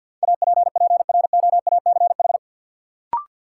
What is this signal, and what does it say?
Morse code, heard as a keyed tone, about 51 wpm, 700 Hz. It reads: APPROACH
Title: Sentences from Top 1400 Words - New Word Drill Alt - 50wpm